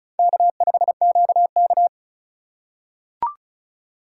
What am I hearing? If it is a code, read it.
Morse code, heard as a keyed tone, about 36 wpm, 700 Hz. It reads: K5QK
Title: USA Call Signs - 4 Characters - 4x Character Spacing - 35wpm